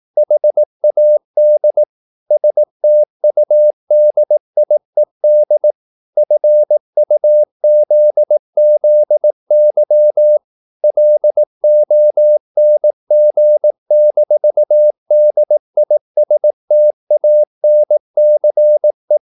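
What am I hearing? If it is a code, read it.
Morse code, heard as a keyed tone, about 18 wpm, 600 Hz. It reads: HAD STUDIED FUZZY LONG-DISTANCE
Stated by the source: Morse practice at 18 wpm, English text, from CW med Gnister.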